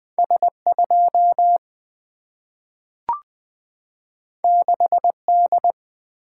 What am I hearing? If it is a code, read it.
Morse code, heard as a keyed tone, about 20 wpm, 700 Hz. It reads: S2 6D
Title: Single-Letter Single-Number 20wpm